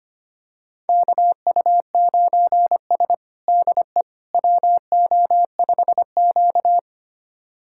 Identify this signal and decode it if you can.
Morse code, heard as a keyed tone, about 25 wpm, 700 Hz. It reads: KU9S DE WO5Q
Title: USA Callsign DE Callsign - 4 Characters - 25wpm